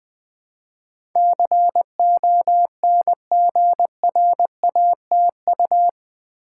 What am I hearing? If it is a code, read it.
Morse code, heard as a keyed tone, about 20 wpm, 700 Hz. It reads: CONGRATU
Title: Build A Word - 4 Letters At A Time - 12-Letter Words - 20wpm